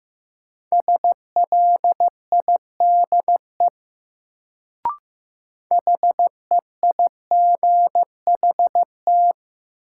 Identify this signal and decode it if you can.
Morse code, heard as a keyed tone, about 15 wpm, 700 Hz. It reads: SLIDE HEIGHT
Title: Sentences from Top 1600 Words - New Word Drill Alt - 15wpm